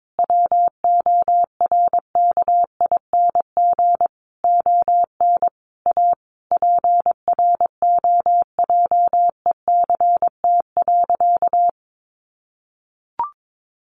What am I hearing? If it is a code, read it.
Morse code, heard as a keyed tone, about 22 wpm, 700 Hz. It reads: WORKING ON A PROJECT.
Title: Sentences from Top 1400 Words 22wpm